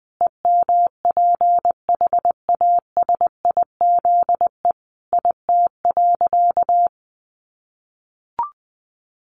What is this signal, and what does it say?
Morse code, heard as a keyed tone, about 20 wpm, 700 Hz. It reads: EMPHASIZE IT.
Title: Sentences from Top 1500 Words 20wpm